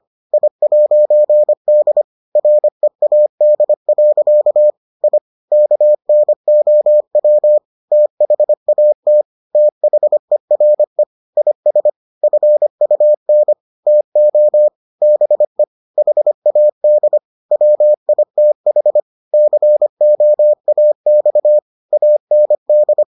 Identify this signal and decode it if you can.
Morse code, heard as a keyed tone, about 25 wpm, 600 Hz. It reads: I'D READ. I KNOW THAT THERE IS FUN TO BE HAD WITH COAX AND